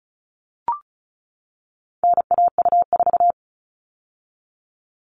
Morse code, heard as a keyed tone, about 34 wpm, 700 Hz. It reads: NAU4
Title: All Cut Numbers - 4-Digit 35wpm